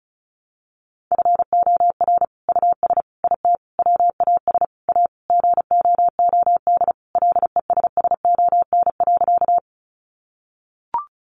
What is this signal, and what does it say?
Morse code, heard as a keyed tone, about 34 wpm, 700 Hz. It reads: FOR US IT WAS A GOOD LESSON.
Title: Sentences from Top 1200 Words 35wpm